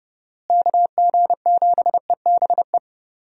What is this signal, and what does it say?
Morse code, heard as a keyed tone, about 32 wpm, 700 Hz. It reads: KG7EBE
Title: USA Call Signs - 6 Characters - Repeated 3x - 30wpm